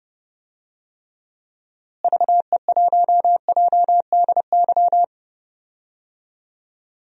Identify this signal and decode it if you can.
Morse code, heard as a keyed tone, about 30 wpm, 700 Hz. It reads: VE1JDY
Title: DXCC-100 Call Signs - 6 Characters - Repeated 2x - 30wpm